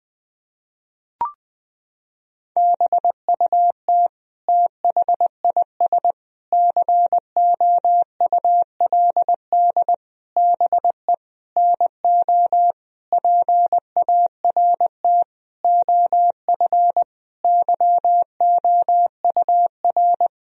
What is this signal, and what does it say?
Morse code, heard as a keyed tone, about 20 wpm, 700 Hz. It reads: BUT THIS COULD BE NO PART OF YOUR
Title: Sentences from Top 1000 Words - Review Part 2 - 20wpm